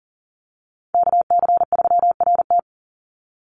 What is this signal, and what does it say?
Morse code, heard as a keyed tone, about 40 wpm, 700 Hz. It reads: KC3RT
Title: NA SOTA Chaser Callsigns 40wpm